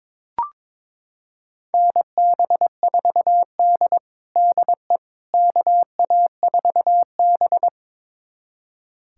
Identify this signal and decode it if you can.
Morse code, heard as a keyed tone, about 22 wpm, 700 Hz. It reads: NB4D DE KA4B
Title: USA Callsign DE Callsign - 4 Characters - 22wpm